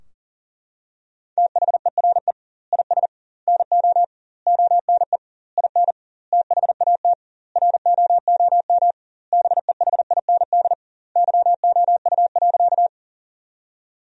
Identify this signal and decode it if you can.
Morse code, heard as a keyed tone, about 40 wpm, 700 Hz. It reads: THERE IS NO ONE IN THAT ROOM BEHIND YOU.
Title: 2x Word-Spacing - Sentences from Top 500 Words - 40wpm